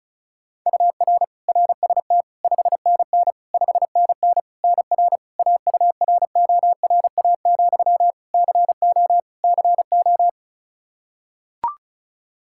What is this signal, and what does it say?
Morse code, heard as a keyed tone, about 36 wpm, 700 Hz. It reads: UR RST 5NN 5NN NR AURORA, CO CO
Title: Practice QSO - Exchange 1 - Beginning - 35wpm